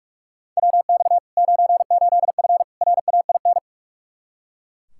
Morse code, heard as a keyed tone, about 42 wpm, 700 Hz. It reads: WX 98F RAIN
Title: Practice QSO - Exchange 2 - Info - 45wpm